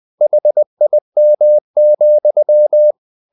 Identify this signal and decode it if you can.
Morse code, heard as a keyed tone, about 20 wpm, 600 Hz. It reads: HIM,
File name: the old man and the sea part 2 20wpm